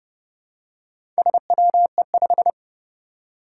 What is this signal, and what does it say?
Morse code, heard as a keyed tone, about 30 wpm, 700 Hz. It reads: SWE5
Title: Four Challenging Letter-Number Combinations 30wpm